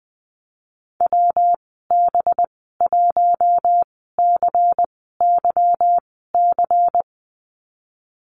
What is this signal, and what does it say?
Morse code, heard as a keyed tone, about 20 wpm, spaced as for 15 wpm, 700 Hz. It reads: WB1CYC
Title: 238 Lesson 38g - Letter q - Call Signs 20wpm